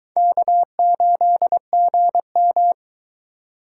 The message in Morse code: K8GM

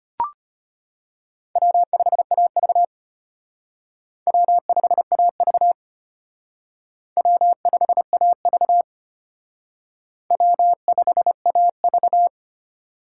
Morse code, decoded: W5AV W5AV W5AV W5AV

